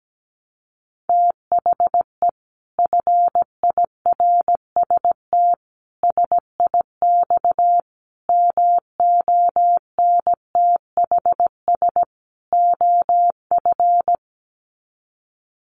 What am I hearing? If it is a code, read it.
Morse code, heard as a keyed tone, about 17 wpm, 700 Hz. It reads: THE FIRST SIX MONTHS OF
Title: Sets of 5 Words Encore2 17wpm